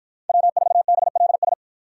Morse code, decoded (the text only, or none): W4BLS